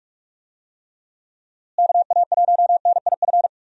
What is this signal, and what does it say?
Morse code, heard as a keyed tone, about 42 wpm, 700 Hz. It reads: KA1NIF
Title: USA Call Signs - 6 Characters - Repeated 3x - 45wpm